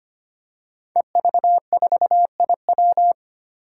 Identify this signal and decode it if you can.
Morse code, heard as a keyed tone, about 25 wpm, 700 Hz. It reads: EV4IW
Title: Five Challenging Letter-Number Combinations 25wpm